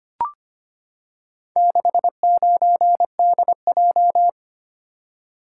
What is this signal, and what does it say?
Morse code, heard as a keyed tone, about 25 wpm, 700 Hz. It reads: 69DJ
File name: practice / 25wpm / Four Letter-Number Combinations - 25wpm